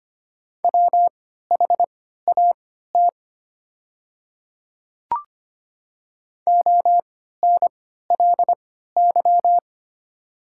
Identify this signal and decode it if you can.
Morse code, heard as a keyed tone, about 25 wpm, spaced as for 14 wpm, 700 Hz. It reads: WHAT ONLY
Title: 3x Character Spacing - Top 200 Words - 25wpm